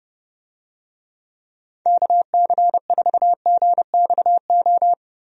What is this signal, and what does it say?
Morse code, heard as a keyed tone, about 30 wpm, 700 Hz. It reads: KC4GXO
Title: USA Call Signs - 6 Characters - Repeated 3x - 30wpm